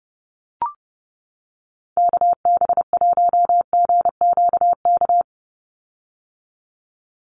Morse code, heard as a keyed tone, about 30 wpm, 700 Hz. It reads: KB1GQK
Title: USA Call Signs 30wpm